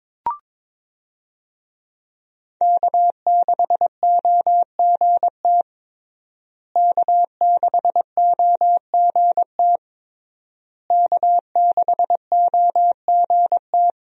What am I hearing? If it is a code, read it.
Morse code, heard as a keyed tone, about 22 wpm, 700 Hz. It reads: K6OGT K6OGT K6OGT
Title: USA Call Signs - 5 Characters - Repeated 3x - 22wpm